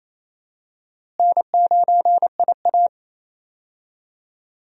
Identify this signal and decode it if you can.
Morse code, heard as a keyed tone, about 28 wpm, 700 Hz. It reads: N9IA
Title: USA Call Signs - 4 Characters - Repeated 2x - 28wpm